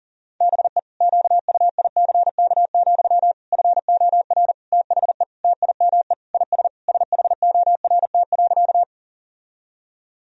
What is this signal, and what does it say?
Morse code, heard as a keyed tone, about 37 wpm, 700 Hz. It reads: BE QUICK, FOR THE TIME IS SHORT.
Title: Sentences from Top 1000 Words - Review Part 2 - 40wpm